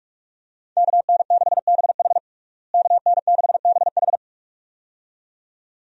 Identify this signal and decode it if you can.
Morse code, heard as a keyed tone, about 47 wpm, 700 Hz. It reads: KN6BH KN6BH
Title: USA Call Signs - 5 Characters - Repeated 2x - 45wpm